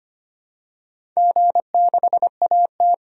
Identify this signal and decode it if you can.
Morse code, heard as a keyed tone, about 25 wpm, 700 Hz. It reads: G6AT
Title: DXCC-100 Call Signs - 4 Characters - 25wpm